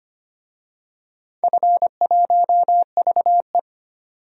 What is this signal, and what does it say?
Morse code, heard as a keyed tone, about 25 wpm, 700 Hz. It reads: F1VE
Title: DXCC-100 Call Signs - 4 Characters - 25wpm